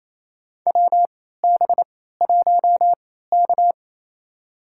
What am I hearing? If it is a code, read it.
Morse code, heard as a keyed tone, about 29 wpm, spaced as for 16 wpm, 700 Hz. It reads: WB1K
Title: USA Call Signs - 4 Characters - 3x Character Spacing - 28wpm